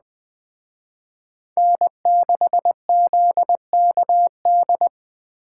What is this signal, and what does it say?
Morse code, heard as a keyed tone, about 20 wpm, 700 Hz. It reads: N6ZKD